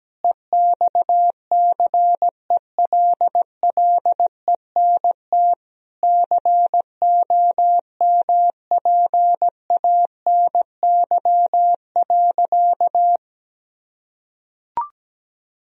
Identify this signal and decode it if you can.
Morse code, heard as a keyed tone, about 17 wpm, 700 Hz. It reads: EXCELLENT COMPANY.
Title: Sentences from Top 1000 Words - Review Part 3 - 17wpm